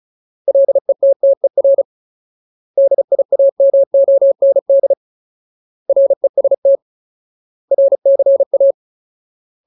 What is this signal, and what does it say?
Morse code, heard as a keyed tone, about 34 wpm, 550 Hz. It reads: LETTER DIAMOND REST RCA
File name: Ultimate Word List / 35WPM Ultimate